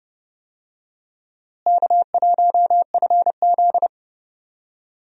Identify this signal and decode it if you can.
Morse code, heard as a keyed tone, about 32 wpm, 700 Hz. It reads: K1FZ
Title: USA Call Signs - 4 Characters - Repeated 3x - 30wpm